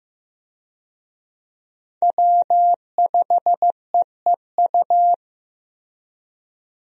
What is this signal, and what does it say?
Morse code, heard as a keyed tone, about 15 wpm, 700 Hz. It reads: W5EEU
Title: USA Call Signs - 5 Characters - Repeated 2x - 15wpm